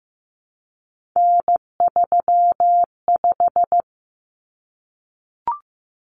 Morse code, heard as a keyed tone, about 15 wpm, 700 Hz. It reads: N35
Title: Common Cut Numbers - 3-Digit 15wpm